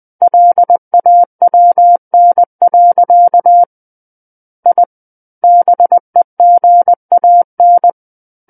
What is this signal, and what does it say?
Morse code, heard as a keyed tone, about 20 wpm, 700 Hz. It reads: LAWN. I BEGAN
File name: 20 WPM